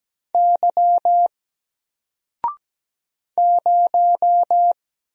Morse code, heard as a keyed tone, about 17 wpm, 700 Hz. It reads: Y 0
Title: Single Letter-Number Alt 17wpm